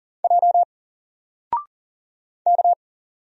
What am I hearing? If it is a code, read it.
Morse code, heard as a keyed tone, about 41 wpm, 700 Hz. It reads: J K
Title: Single Character (Letter:Number:Punctuation:Prosign) 40wpm